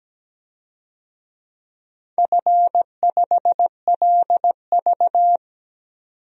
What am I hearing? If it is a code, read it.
Morse code, heard as a keyed tone, about 17 wpm, 700 Hz. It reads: F5LV
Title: DXCC-100 Call Signs - 4 Characters - 17wpm